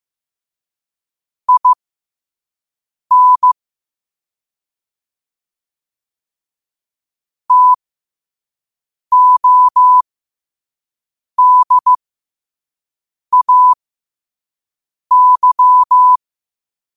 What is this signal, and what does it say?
Morse code, heard as a keyed tone, about 15 wpm, spaced as for 5 wpm, 1000 Hz. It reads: IN TODAY
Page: qotd-at-5-wpm